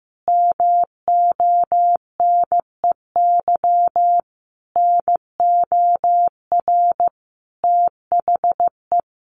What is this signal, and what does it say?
Morse code, heard as a keyed tone, about 15 wpm, 700 Hz. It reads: MONEY NOR THE
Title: Sentences from Top 1500 Words 15wpm